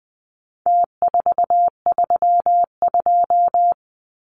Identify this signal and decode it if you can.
Morse code, heard as a keyed tone, about 20 wpm, 700 Hz. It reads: T432
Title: Numbers - Contest Style - 4-Digit 20wpm